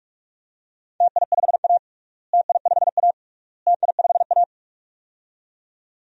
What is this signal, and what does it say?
Morse code, heard as a keyed tone, about 41 wpm, 690 Hz. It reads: TI5A TI5A TI5A
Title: DXCC-100 Call Signs - 4 Characters - Repeated 3x - 45wpm